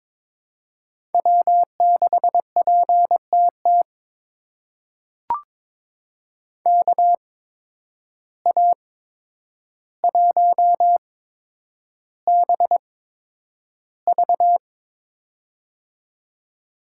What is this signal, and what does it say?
Morse code, heard as a keyed tone, about 23 wpm, spaced as for 10 wpm, 700 Hz. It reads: W6PTT KA1BV
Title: USA Call Signs - 5 Characters - 8x Character Spacing - 22wpm